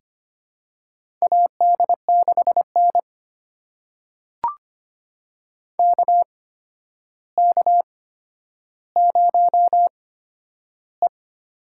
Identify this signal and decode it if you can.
Morse code, heard as a keyed tone, about 25 wpm, spaced as for 11 wpm, 700 Hz. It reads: AD6N KK0E
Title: USA Call Signs - 4 Characters - 8x Character Spacing - 25wpm